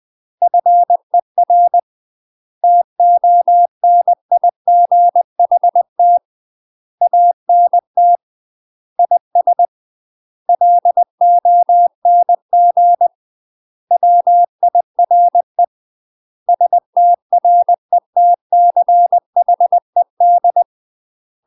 Morse code, heard as a keyed tone, about 20 wpm, 700 Hz. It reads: FER TONIGHT ANT IS LONG WIRE STRETCHED
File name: UK MASTER QSOs TRAINING/20WPM